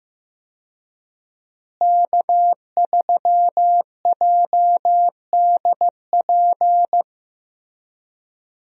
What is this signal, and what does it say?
Morse code, heard as a keyed tone, about 15 wpm, 700 Hz. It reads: K3JDP
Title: USA Call Signs - 5 Characters - Repeated 2x - 15wpm